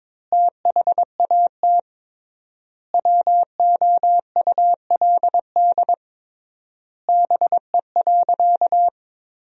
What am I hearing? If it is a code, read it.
Morse code, heard as a keyed tone, about 22 wpm, 700 Hz. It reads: THAT WOULD BE.
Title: 3x Word-Spacing - Sentences from Top 200 Words - 22wpm